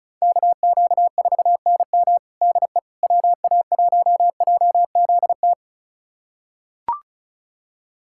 Morse code, heard as a keyed tone, about 36 wpm, 700 Hz. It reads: KQ4NM DE WA1JZT
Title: USA Callsign DE Callsign 35wpm